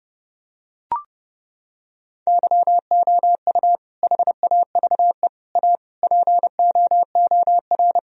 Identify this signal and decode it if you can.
Morse code, heard as a keyed tone, about 32 wpm, 700 Hz. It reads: YOU HAVE A POOR
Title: Sentences from Top 1000 Words - Review Part 3 - 30wpm